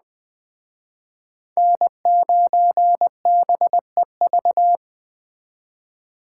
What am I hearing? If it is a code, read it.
Morse code, heard as a keyed tone, about 20 wpm, 700 Hz. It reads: N9BEV
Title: Speed Racing - USA 1x3 Call Signs 17wpm